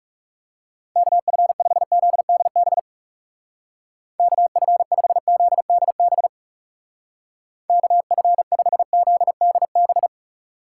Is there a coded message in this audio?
Yes: Morse code from 1.0 to 10.1 s, heard as a keyed tone, about 39 wpm, 700 Hz. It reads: KF5ZDB KF5ZDB KF5ZDB